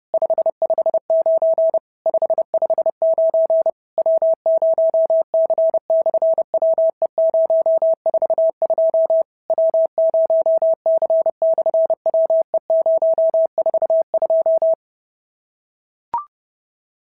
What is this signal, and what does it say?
Morse code, heard as a keyed tone, about 30 wpm, 650 Hz. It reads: 559 559 W0C/WE042 W0C/WE042
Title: SOTA QSO Practice 30wpm